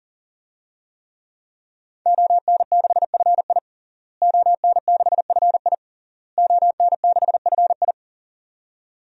Morse code, heard as a keyed tone, about 40 wpm, 700 Hz. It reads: ON6FI ON6FI ON6FI
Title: DXCC-100 Call Signs - 5 Characters - Repeated 3x - 40wpm